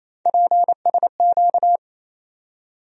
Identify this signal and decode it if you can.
Morse code, heard as a keyed tone, about 28 wpm, 700 Hz. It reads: PSQ
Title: USA Call Sign Postfixes - 3 Letters 28wpm